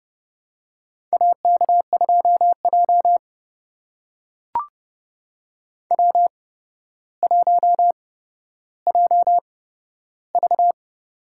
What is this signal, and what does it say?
Morse code, heard as a keyed tone, about 29 wpm, spaced as for 13 wpm, 700 Hz. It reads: AK2J W1JV
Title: USA Call Signs - 4 Characters - 8x Character Spacing - 30wpm